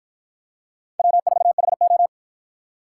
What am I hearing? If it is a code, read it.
Morse code, heard as a keyed tone, about 48 wpm, 700 Hz. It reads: W4HO